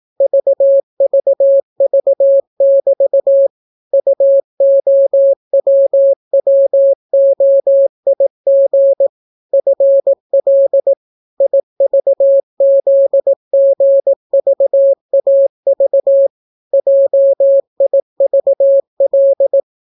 Morse code, heard as a keyed tone, about 18 wpm, 550 Hz. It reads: VVV<BT> UOWWOIG FL IVZGVAV JIVL